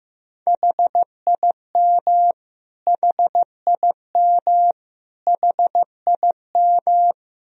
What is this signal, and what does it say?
Morse code, heard as a keyed tone, about 15 wpm, 700 Hz. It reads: HIM HIM HIM
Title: Top 100 Words Repeated 15wpm